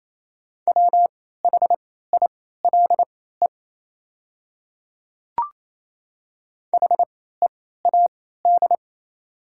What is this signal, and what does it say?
Morse code, heard as a keyed tone, about 28 wpm, spaced as for 16 wpm, 700 Hz. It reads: WHILE HEAD
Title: 3x Character Spacing - Top 300 Words - 28wpm